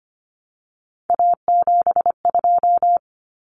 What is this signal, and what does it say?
Morse code, heard as a keyed tone, about 25 wpm, 700 Hz. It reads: A72
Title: Common Cut Numbers - 3-Digit 25wpm